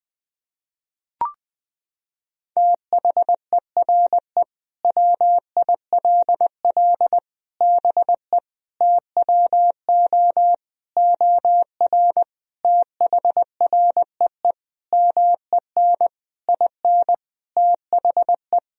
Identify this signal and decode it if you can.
Morse code, heard as a keyed tone, about 21 wpm, 700 Hz. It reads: THERE WILL BE TWO OR THREE MEN IN THE
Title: Sentences from Top 1000 Words - Review Part 3 - 20wpm